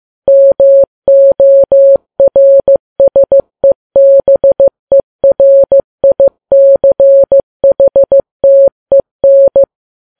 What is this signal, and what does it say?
Morse code, heard as a keyed tone, about 15 wpm, 550 Hz. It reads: MORSEBERICHTEN
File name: Files Web tls / Morse on9fv / morse15kps